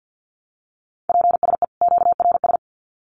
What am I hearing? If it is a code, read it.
Morse code, heard as a keyed tone, about 50 wpm, 700 Hz. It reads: PSE QRS
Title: Common QSO Phrases 50wpm